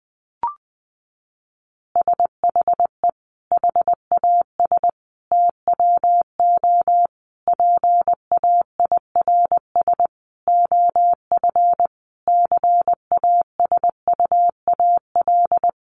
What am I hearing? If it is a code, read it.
Morse code, heard as a keyed tone, about 20 wpm, 700 Hz. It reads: SHE HAS TWO PAIRS OF CASUAL
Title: Additional Easy Phrases 20wpm